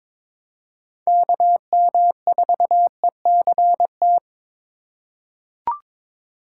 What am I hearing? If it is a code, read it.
Morse code, heard as a keyed tone, about 22 wpm, 700 Hz. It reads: KM4ECT
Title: USA Call Signs - 3x Character Spacing - 22wpm